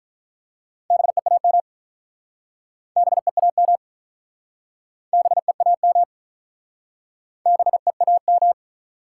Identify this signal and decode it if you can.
Morse code, heard as a keyed tone, about 43 wpm, 700 Hz. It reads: BEAM BEAM BEAM BEAM
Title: Speed-Racing Top 90 QSO Elements 35wpm